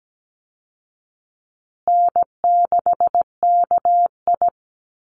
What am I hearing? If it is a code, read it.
Morse code, heard as a keyed tone, about 17 wpm, 700 Hz. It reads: N6KI
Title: Easy Call Signs Repeated 3x - 17wpm